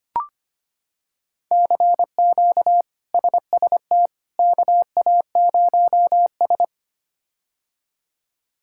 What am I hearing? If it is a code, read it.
Morse code, heard as a keyed tone, about 25 wpm, 700 Hz. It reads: CQ SST KA0S
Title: K1USN SST QSO Practice - 25wpm